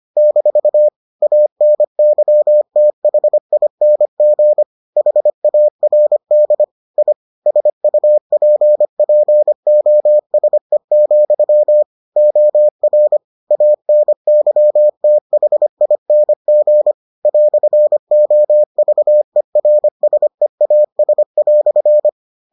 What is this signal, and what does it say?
Morse code, heard as a keyed tone, about 25 wpm, 600 Hz. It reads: - ANYTHING HARD I SUPPOSE, OR ANYTHING "OVERSEAS"